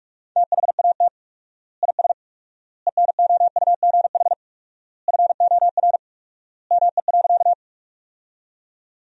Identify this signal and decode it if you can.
Morse code, heard as a keyed tone, about 46 wpm, 700 Hz. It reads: THAT IS ENOUGH FOR ME.
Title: 4x Word-Spacing - Sentences from Top 400 Words - 45wpm